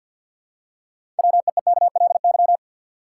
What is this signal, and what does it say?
Morse code, heard as a keyed tone, about 48 wpm, 700 Hz. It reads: WEEKLY
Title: Six Letter Words - 50wpm